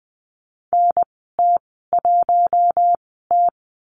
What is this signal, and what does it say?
Morse code, heard as a keyed tone, about 20 wpm, spaced as for 14 wpm, 700 Hz. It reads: NT1T
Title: 048 Lesson 9g - Number 4 - Call Signs 20wpm